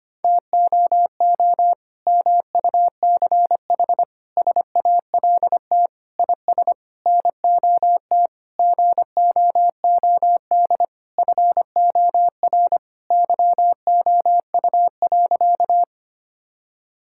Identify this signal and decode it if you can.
Morse code, heard as a keyed tone, about 25 wpm, 700 Hz. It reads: TOO MUCH SALT IS NOT GOOD FOR YOU.
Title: Sentences from Top 1000 Words - Review Part 5 - 25wpm